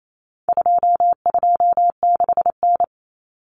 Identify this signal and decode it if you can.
Morse code, heard as a keyed tone, about 28 wpm, 700 Hz. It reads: 226N
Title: Common Cut Numbers - 4-Digit 28wpm